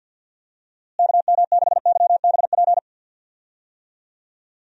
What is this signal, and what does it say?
Morse code, heard as a keyed tone, about 49 wpm, 700 Hz. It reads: KM6YBP
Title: USA Call Signs - 6 Characters - Repeated 3x - 50wpm